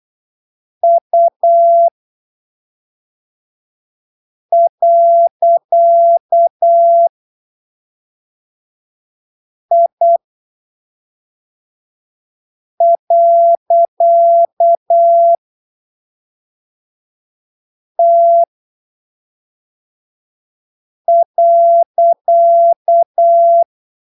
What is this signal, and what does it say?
Morse code, heard as a keyed tone, about 8 wpm, spaced as for 3 wpm, 680 Hz. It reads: U.I.T.
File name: arquivos / texto / texto 07 08ppm